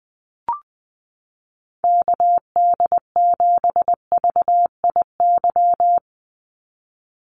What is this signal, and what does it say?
Morse code, heard as a keyed tone, about 20 wpm, 700 Hz. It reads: KD7VIY
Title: USA Call Signs 20wpm